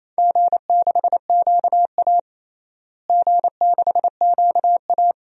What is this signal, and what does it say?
Morse code, heard as a keyed tone, about 28 wpm, 700 Hz. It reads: G6QA G6QA
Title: DXCC-100 Call Signs - 4 Characters - Repeated 3x - 28wpm